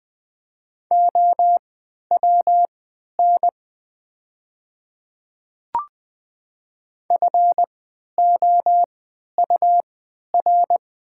Morse code, heard as a keyed tone, about 20 wpm, spaced as for 11 wpm, 700 Hz. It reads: OWN FOUR
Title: 3x Character Spacing - Top 500 Words - 20wpm